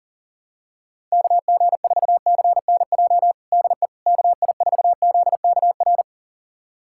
Morse code, heard as a keyed tone, about 40 wpm, 700 Hz. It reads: KG4CNJ DE KI4ZKR
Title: USA Callsign DE Callsign - 6 Characters - 40wpm